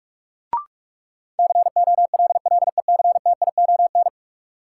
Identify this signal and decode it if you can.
Morse code, heard as a keyed tone, about 45 wpm, 700 Hz. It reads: COLLECTION
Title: Sentences from Top 1200 Words - New Word Drill Alt - 45wpm